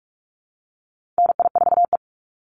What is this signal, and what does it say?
Morse code, heard as a keyed tone, about 44 wpm, 700 Hz. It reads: NI4E
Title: Call Signs Repeated 3x - 45wpm